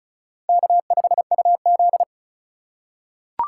Morse code, heard as a keyed tone, about 36 wpm, 700 Hz. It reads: K5UZ
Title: USA Call Signs - 4 Characters - 3x Character Spacing - 35wpm